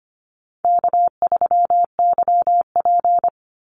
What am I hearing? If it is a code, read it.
Morse code, heard as a keyed tone, about 25 wpm, 700 Hz. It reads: K3YP
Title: Call Signs - 25wpm